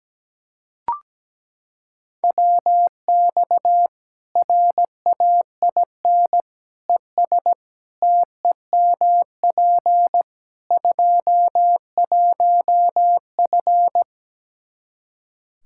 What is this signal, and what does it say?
Morse code, heard as a keyed tone, about 17 wpm, 700 Hz. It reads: WX RAIN ES TEMP 21F
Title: Practice QSO - Exchange 2 - Info - 17wpm